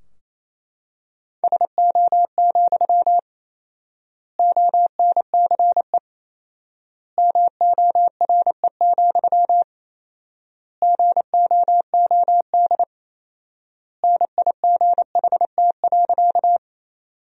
Morse code, heard as a keyed tone, about 28 wpm, 700 Hz. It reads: SO, ONCE MORE, GOOD NIGHT.
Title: 4x Word-Spacing - Sentences from Top 300 Words - 28wpm